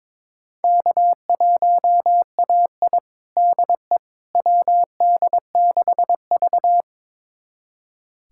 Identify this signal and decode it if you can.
Morse code, heard as a keyed tone, about 22 wpm, 700 Hz. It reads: K1AI DE WD6V
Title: USA Callsign DE Callsign - 4 Characters - 22wpm